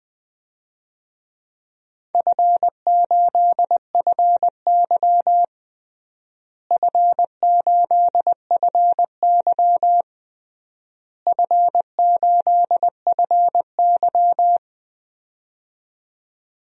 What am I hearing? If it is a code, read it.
Morse code, heard as a keyed tone, about 20 wpm, 700 Hz. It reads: F8FY F8FY F8FY
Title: DXCC-100 Call Signs - 4 Characters - Repeated 3x - 20wpm